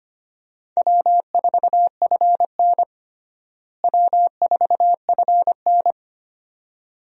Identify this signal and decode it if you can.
Morse code, heard as a keyed tone, about 25 wpm, 700 Hz. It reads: W4FN W4FN
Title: USA Call Signs - 4 Characters - Repeated 2x - 25wpm